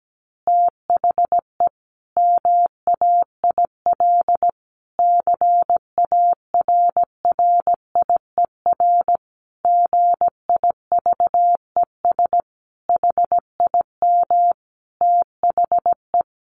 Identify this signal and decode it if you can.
Morse code, heard as a keyed tone, about 17 wpm, 700 Hz. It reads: THE MAIL CARRIER GIVES HIM THE
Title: Additional Easy Phrases 17wpm